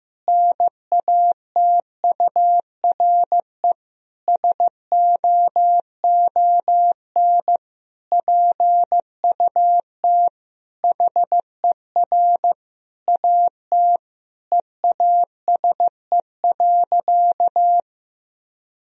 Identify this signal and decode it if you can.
Morse code, heard as a keyed tone, about 15 wpm, 700 Hz. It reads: NATURE SOON PUT HER AT EASE.
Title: Sentences from Top 1000 Words - Review Part 1 - 15wpm